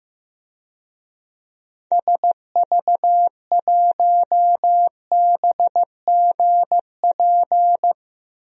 Morse code, heard as a keyed tone, about 15 wpm, 700 Hz. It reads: SV1BGP
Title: DXCC-100 Call Signs - 6 Characters - 15wpm